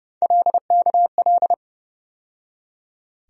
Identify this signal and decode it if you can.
Morse code, heard as a keyed tone, about 30 wpm, 700 Hz. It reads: LKL
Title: USA Call Sign Postfixes - 3 Letters 30wpm